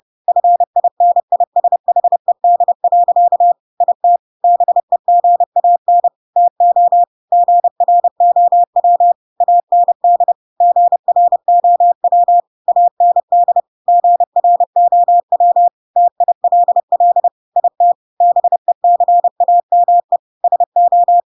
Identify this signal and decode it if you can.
Morse code, heard as a keyed tone, about 30 wpm, 700 Hz. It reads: FINISHED. IT BEGAN TO GROW AND GROW AND GROW TILL IT BECAME SO
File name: Pinocchio / Pinocchio ch3 30 30 ik1hge